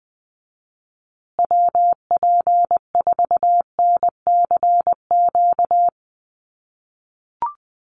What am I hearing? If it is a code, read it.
Morse code, heard as a keyed tone, about 20 wpm, 700 Hz. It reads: WP4NCQ
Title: USA Call Signs Repeated 2x 20wpm